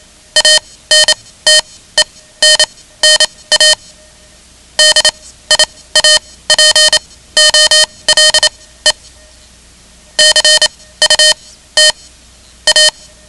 Morse code, heard as a keyed tone, about 27 wpm, spaced as for 17 wpm, 3050 Hz. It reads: ANTENNA DIAPOLE CUT A
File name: ham radio / CodeQuick / Code Quick 10